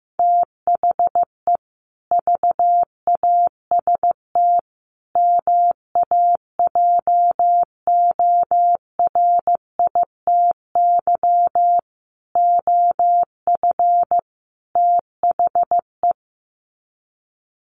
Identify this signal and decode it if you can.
Morse code, heard as a keyed tone, about 15 wpm, 700 Hz. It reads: THE VAST MAJORITY OF THE
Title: Sets of 5 Words Encore 15wpm